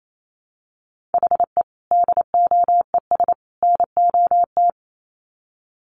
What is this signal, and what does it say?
Morse code, heard as a keyed tone, about 28 wpm, 700 Hz. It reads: HE DOES NOT
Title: Sets of 3 Words 28wpm